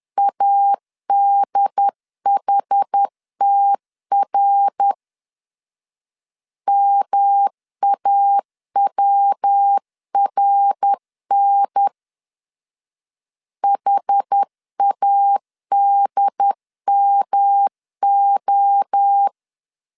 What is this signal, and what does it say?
Morse code, heard as a keyed tone, about 10 wpm, 800 Hz. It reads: ADHTR MAWRN HADMO
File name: CW / DET / 1 3 1500C S A 10wpm 3447